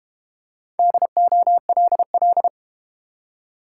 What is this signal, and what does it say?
Morse code, heard as a keyed tone, about 31 wpm, 700 Hz. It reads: DOLL